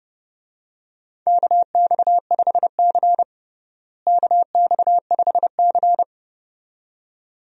Morse code, heard as a keyed tone, about 31 wpm, 700 Hz. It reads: KX5C KX5C
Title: USA Call Signs - 4 Characters - Repeated 2x - 30wpm